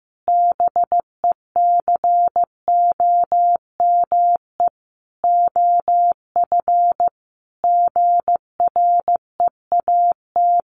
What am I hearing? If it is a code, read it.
Morse code, heard as a keyed tone, about 15 wpm, 700 Hz. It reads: BECOME OF GREAT
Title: Sentences from Top 1200 Words 15wpm